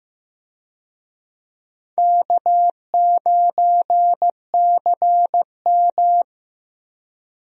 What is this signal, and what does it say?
Morse code, heard as a keyed tone, about 15 wpm, 700 Hz. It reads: K9CM
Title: USA Call Signs - 4 Characters - Repeated 2x - 15wpm